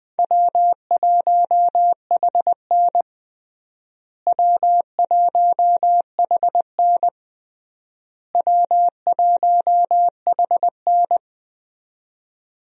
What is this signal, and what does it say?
Morse code, heard as a keyed tone, about 20 wpm, 700 Hz. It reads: W1HN W1HN W1HN
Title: USA Call Signs - 4 Characters - Repeated 3x - 20wpm